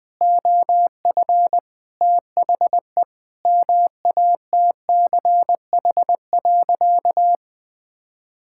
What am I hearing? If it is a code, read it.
Morse code, heard as a keyed tone, about 20 wpm, 700 Hz. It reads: OF THE MATCH.
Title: Sentences with 5-Word Phrases - Spoken Beginning - 20wpm